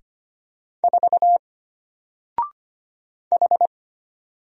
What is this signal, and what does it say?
Morse code, heard as a keyed tone, about 25 wpm, 700 Hz. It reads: 4 H
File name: Single-Letter-Number-25wpm